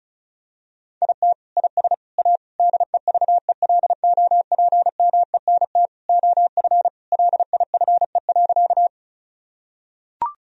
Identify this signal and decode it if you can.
Morse code, heard as a keyed tone, about 36 wpm, 700 Hz. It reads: IT IS A DEVELOPMENT OF LIFE.
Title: Sentences from Top 1000 Words - Review Part 4 - 35wpm